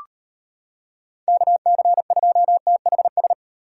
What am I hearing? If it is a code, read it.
Morse code, heard as a keyed tone, about 33 wpm, 700 Hz. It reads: KC2THS